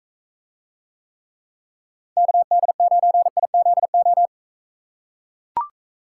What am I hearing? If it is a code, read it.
Morse code, heard as a keyed tone, about 42 wpm, 700 Hz. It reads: KD9IZO